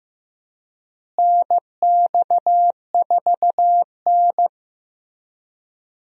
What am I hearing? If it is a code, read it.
Morse code, heard as a keyed tone, about 15 wpm, 700 Hz. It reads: NX4N
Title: USA Call Signs - 4 Characters - Repeated 3x - 15wpm